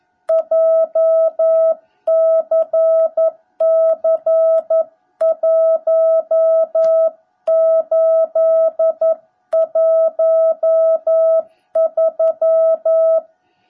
Morse code, heard as a keyed tone, about 11 wpm, 660 Hz. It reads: JCC1813
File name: jh8cbh / 2021cw / cw17b